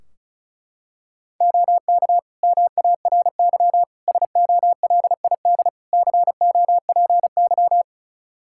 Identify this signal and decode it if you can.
Morse code, heard as a keyed tone, about 34 wpm, 700 Hz. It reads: OK MARY SOLID COPY
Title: Practice QSO - Exchange 3 - More Info - 35wpm